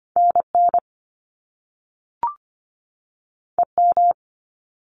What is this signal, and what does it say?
Morse code, heard as a keyed tone, about 25 wpm, 700 Hz. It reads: NN EM
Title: Top Two Letter Combinations in Words 25wpm